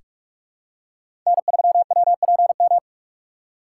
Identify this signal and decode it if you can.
Morse code, heard as a keyed tone, about 45 wpm, 700 Hz. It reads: N3WPM
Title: USA Call Signs - 4x Character Spacing - 45wpm